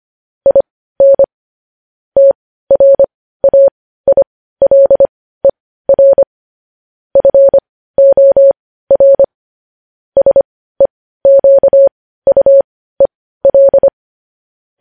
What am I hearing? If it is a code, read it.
Morse code, heard as a keyed tone, about 25 wpm, spaced as for 15 wpm, 550 Hz. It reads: IN TRAILER FOR SEQUEL